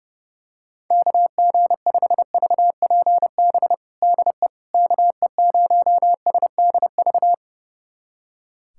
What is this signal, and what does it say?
Morse code, heard as a keyed tone, about 30 wpm, 700 Hz. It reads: KG5VPB DE KE0SDV
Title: USA Callsign DE Callsign 30wpm